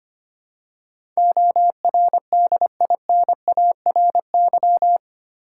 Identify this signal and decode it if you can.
Morse code, heard as a keyed tone, about 25 wpm, 700 Hz. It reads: ORDINARY
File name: 8 Letter Words - First 3 Letters Given - 25wpm